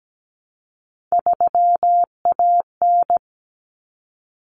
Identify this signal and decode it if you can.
Morse code, heard as a keyed tone, about 17 wpm, 690 Hz. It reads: 3AN
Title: Common Cut Numbers - 3-Digit 17wpm